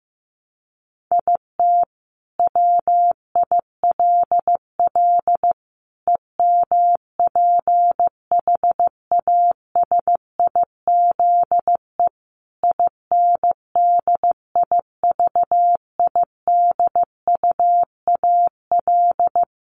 Morse code, heard as a keyed tone, about 15 wpm, 700 Hz. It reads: IT WILL EMPHASIZE INDIVIDUAL
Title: Sentences from Top 1500 Words 15wpm